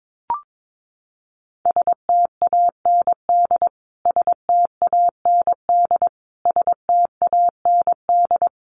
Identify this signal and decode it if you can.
Morse code, heard as a keyed tone, about 22 wpm, 700 Hz. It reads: STAND STAND STAND
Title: Top 500 Words Repeated 22wpm